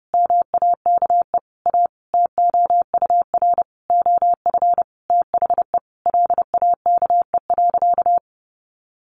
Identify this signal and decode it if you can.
Morse code, heard as a keyed tone, about 30 wpm, 700 Hz. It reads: MAKE A TOUR OF THE LAKE.
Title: Sentences from Top 1400 Words 30wpm